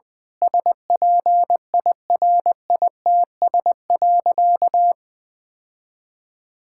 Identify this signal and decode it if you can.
Morse code, heard as a keyed tone, about 20 wpm, 700 Hz. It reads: SPIRITS.
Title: Difficult Sentences with Hints - 20wpm